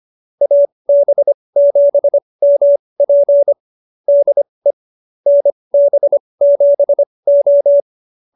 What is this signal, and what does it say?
Morse code, heard as a keyed tone, about 25 wpm, spaced as for 20 wpm, 570 Hz. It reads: AB7MP DE NB7O